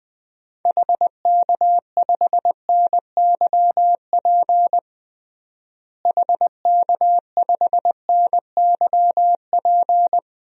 Morse code, heard as a keyed tone, about 20 wpm, 700 Hz. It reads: HK5NYP HK5NYP
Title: DXCC-100 Call Signs - 6 Characters - Repeated 2x - 20wpm